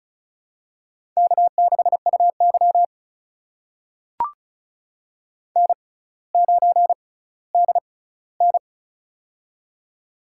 Morse code, heard as a keyed tone, about 34 wpm, spaced as for 18 wpm, 700 Hz. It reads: K6UY N9DN
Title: USA Call Signs - 4 Characters - 6x Character Spacing - 35wpm